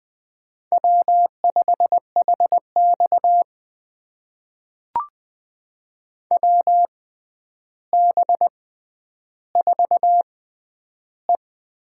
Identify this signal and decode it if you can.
Morse code, heard as a keyed tone, about 20 wpm, spaced as for 10 wpm, 700 Hz. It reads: W5HX WB4E
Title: USA Call Signs - 6x Character Spacing - 20wpm